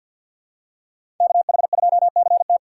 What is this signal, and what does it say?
Morse code, heard as a keyed tone, about 47 wpm, 690 Hz. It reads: KH2CT
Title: USA Call Signs - 6x Character Spacing - 50wpm